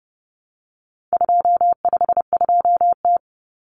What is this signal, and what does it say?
Morse code, heard as a keyed tone, about 30 wpm, 700 Hz. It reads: 252T
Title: Common Cut Numbers - 4-Digit 30wpm